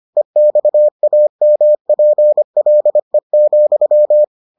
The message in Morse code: EXAMPLE,

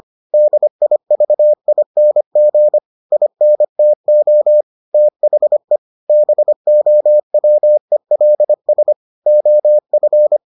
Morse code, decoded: DIVING INTO THE BOWELS OF

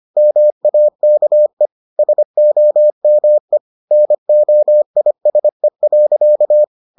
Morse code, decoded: MAKE SOME NOISE.

